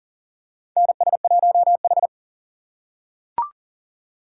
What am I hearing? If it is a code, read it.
Morse code, heard as a keyed tone, about 40 wpm, 700 Hz. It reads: NS1H
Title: USA Call Signs - 4 Characters - 3x Character Spacing - 40wpm